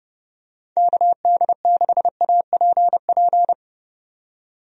Morse code, heard as a keyed tone, about 30 wpm, 700 Hz. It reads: KD6APP
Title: USA Call Signs - 6 Characters - 30wpm